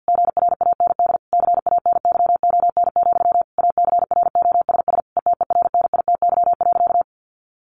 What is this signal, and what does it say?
Morse code, heard as a keyed tone, about 50 wpm, 700 Hz. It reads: GRAND CANYON, ACROSS ETERNITY.
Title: A Sound Of Thunder 50wpm